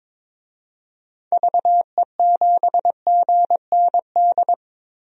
Morse code, heard as a keyed tone, about 22 wpm, 700 Hz. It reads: VE7GND
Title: DXCC-100 Call Signs - 6 Characters - 22wpm